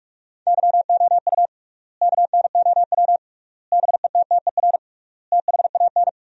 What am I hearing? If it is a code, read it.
Morse code, heard as a keyed tone, about 42 wpm, 700 Hz. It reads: YOU KNOW BETTER THAN
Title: 3x Word-Spacing - Sentences from Top 400 Words - 45wpm